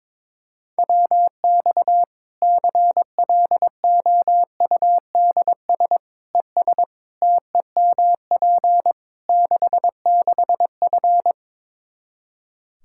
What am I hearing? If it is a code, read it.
Morse code, heard as a keyed tone, about 22 wpm, 700 Hz. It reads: WX CLOUDS ES TEMP 66F
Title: Practice QSO - Exchange 2 - Info - 22wpm